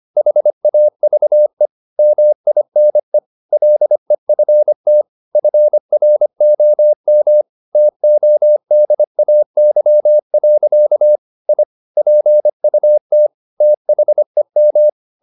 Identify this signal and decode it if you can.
Morse code, heard as a keyed tone, about 25 wpm, 600 Hz. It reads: HAVE MINE LEFT FROM TODAY. I PUT THEM
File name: the old man and the sea part 1 25wpm